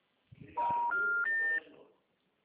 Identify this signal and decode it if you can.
Special information tone.